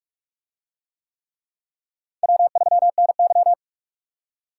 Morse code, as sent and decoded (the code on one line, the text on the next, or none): .-- ...-- -. -.--
W3NY